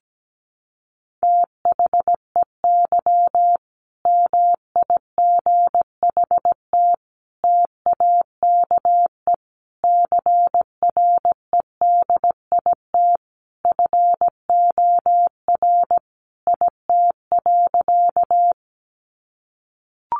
Morse code, heard as a keyed tone, about 17 wpm, 700 Hz. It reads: THEY MIGHT TAKE CREDIT FOR IT.
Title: Sentences from Top 1000 Words 17wpm